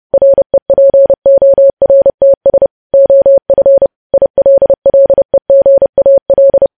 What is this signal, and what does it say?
Morse code, heard as a keyed tone, about 30 wpm, 550 Hz. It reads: REPORTS OF ILLEGAL
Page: News Headlines in Morse Code at 30 WPM